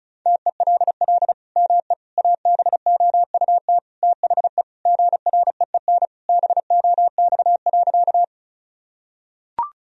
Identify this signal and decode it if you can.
Morse code, heard as a keyed tone, about 36 wpm, 700 Hz. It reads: TELL ME ABOUT THE GREEN BOX.
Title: Sentences from Top 1000 Words - Review Part 2 - 35wpm